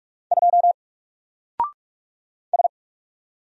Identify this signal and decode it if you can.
Morse code, heard as a keyed tone, about 44 wpm, 700 Hz. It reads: 2 S
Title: Single Character (Letter:Number:Punctuation:Prosign) Alt 45wpm